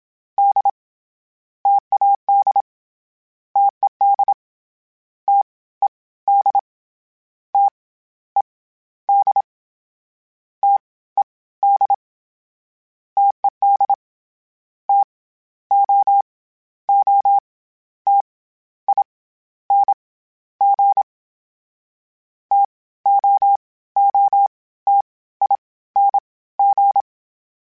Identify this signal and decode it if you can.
Morse code, heard as a keyed tone, about 27 wpm, 800 Hz. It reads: D TAD TED T E D T E D T E D TED T O O T I N G T O O T I N G
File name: Difficult Combos / Difficult Combos-mp3s / T AOEI @25